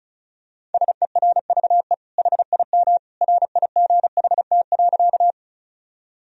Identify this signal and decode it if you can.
Morse code, heard as a keyed tone, about 36 wpm, 700 Hz. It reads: SERVE HIM RIGHT.
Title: Sentences from Top 1000 Words - Review Part 2 - 35wpm